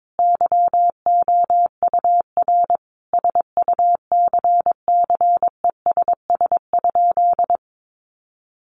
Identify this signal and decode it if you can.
Morse code, heard as a keyed tone, about 22 wpm, 700 Hz. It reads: YOUR SUCCESS?
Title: Sentences from Top 900 Words 22wpm